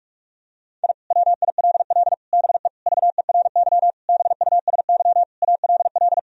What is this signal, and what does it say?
Morse code, heard as a keyed tone, about 44 wpm, 700 Hz. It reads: I WILL BE VERY BUSY ALL
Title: Sentences from Top 1000 Words - Review Part 3 - 45wpm